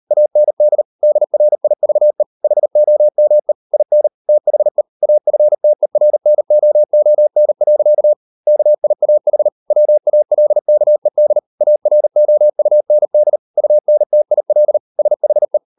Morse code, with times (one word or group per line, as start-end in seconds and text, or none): 0.1-0.8 AND
1.0-2.2 DRIVE
2.4-3.5 HOME
3.7-4.1 IN
4.3-4.8 THE
5.0-8.1 AFTERNOON.
8.5-9.5 KIAH
9.7-11.4 WALKED
11.6-13.4 AROUND
13.6-14.8 UNTIL
15.0-15.6 SHE